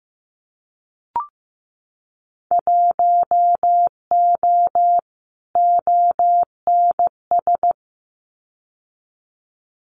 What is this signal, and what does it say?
Morse code, heard as a keyed tone, about 15 wpm, 700 Hz. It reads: 1O ONS
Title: Winter Field Day Exchanges 15wpm